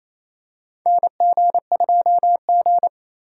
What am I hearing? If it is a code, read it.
Morse code, heard as a keyed tone, about 28 wpm, 700 Hz. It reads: NG2G
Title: USA Call Signs - 4 Characters - 2x Character Spacing - 28wpm